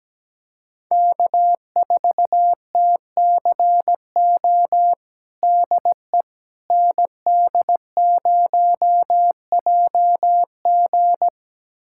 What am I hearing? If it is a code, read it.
Morse code, heard as a keyed tone, about 17 wpm, 700 Hz. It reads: K4TCO DE ND0JG
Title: USA Callsign DE Callsign - 5 Characters - 17wpm